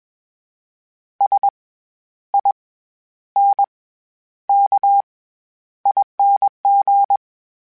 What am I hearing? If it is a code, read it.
Morse code, heard as a keyed tone, about 21 wpm, spaced as for 11 wpm, 800 Hz. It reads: SINKING
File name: Special INGs-mp3s / Special INGs-mp3s / SlowToFaster ing @20